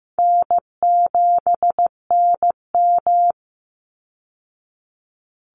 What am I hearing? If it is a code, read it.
Morse code, heard as a keyed tone, about 15 wpm, 700 Hz. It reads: N7NM
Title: Call Signs Repeated 3x - 15wpm